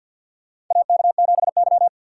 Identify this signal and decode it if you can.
Morse code, heard as a keyed tone, about 49 wpm, 700 Hz. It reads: AK7Y
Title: USA Call Signs - 4 Characters - 3x Character Spacing - 50wpm